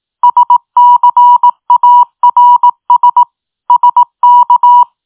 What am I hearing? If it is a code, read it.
Morse code, heard as a keyed tone, about 18 wpm, 970 Hz. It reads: SCARS SK